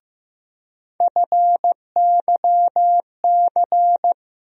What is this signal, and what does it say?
Morse code, heard as a keyed tone, about 15 wpm, 700 Hz. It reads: FYC
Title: USA Call Sign Postfixes - 3 Letters 15wpm